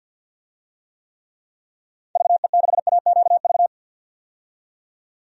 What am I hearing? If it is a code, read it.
Morse code, heard as a keyed tone, about 49 wpm, 700 Hz. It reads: VE6AQV
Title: DXCC-100 Call Signs - 6 Characters - 50wpm